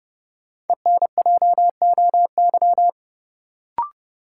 Morse code, heard as a keyed tone, about 30 wpm, 700 Hz. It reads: ENJOY
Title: Sentences from Top 1000 Words - Review Part 4 - New Word Drill - 30wpm